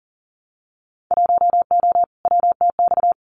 Morse code, heard as a keyed tone, about 40 wpm, 700 Hz. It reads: 1O WTX
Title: Winter Field Day Exchanges Encore 40wpm